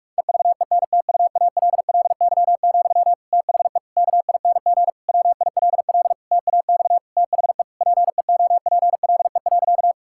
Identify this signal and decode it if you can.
Morse code, heard as a keyed tone, about 41 wpm, 700 Hz. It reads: EVENTUALLY, THE KING WILL TAX THE PEOPLE.
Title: Sentences from Top 1000 Words - Review Part 6 - 45wpm